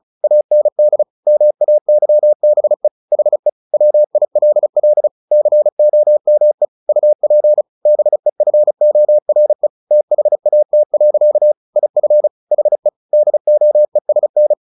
Morse code, heard as a keyed tone, about 36 wpm, 600 Hz. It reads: AND MAYBE HE WILL COME UP BEFORE THAT. IF HE DOESN
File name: the old man and the sea / the old man and the sea 1 35wpm